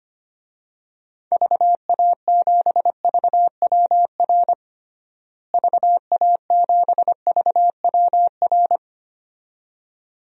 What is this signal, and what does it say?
Morse code, heard as a keyed tone, about 25 wpm, 700 Hz. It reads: VA7VWR VA7VWR
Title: DXCC-100 Call Signs - 6 Characters - Repeated 2x - 25wpm